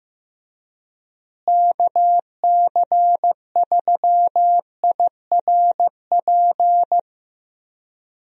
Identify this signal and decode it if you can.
Morse code, heard as a keyed tone, about 15 wpm, 700 Hz. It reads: KC3IRP
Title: USA Call Signs - 6 Characters - Repeated 3x - 15wpm